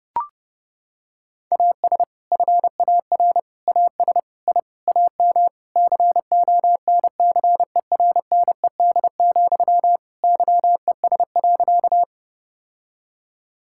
Morse code, heard as a keyed tone, about 32 wpm, 700 Hz. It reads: AS FAR AS I AM CONCERNED, YES.
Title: Sentences from Top 1000 Words - Review Part 6 - 30wpm